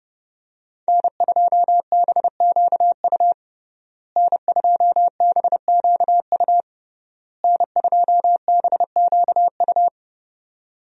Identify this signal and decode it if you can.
Morse code, heard as a keyed tone, about 30 wpm, 700 Hz. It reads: N2BQU N2BQU N2BQU
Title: USA Call Signs - 5 Characters - Repeated 3x - 30wpm